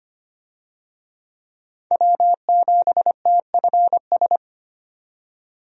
Morse code, heard as a keyed tone, about 25 wpm, 700 Hz. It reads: W7TFS
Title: USA Call Signs - 5 Characters - 25wpm